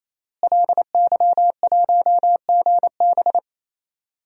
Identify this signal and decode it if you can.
Morse code, heard as a keyed tone, about 28 wpm, 700 Hz. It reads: LY1GB
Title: DXCC-100 Call Signs - 5 Characters - Repeated 3x - 28wpm